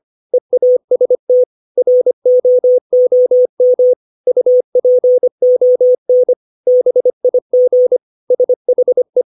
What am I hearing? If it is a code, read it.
Morse code, heard as a keyed tone, about 25 wpm, 500 Hz. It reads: EAST ROOM UPON BIG SHE
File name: 632 words / 632 Words 25WPM